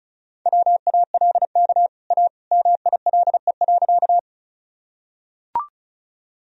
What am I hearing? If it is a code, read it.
Morse code, heard as a keyed tone, about 34 wpm, 700 Hz. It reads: WALK A MILE.
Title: Sentences from Top 1000 Words - Review Part 1 - 35wpm